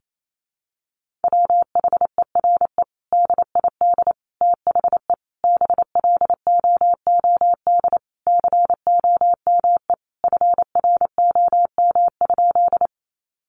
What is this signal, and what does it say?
Morse code, heard as a keyed tone, about 28 wpm, 700 Hz. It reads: WHERE DID THE BLOOD COME FROM?
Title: Sentences from Top 1100 Words 28wpm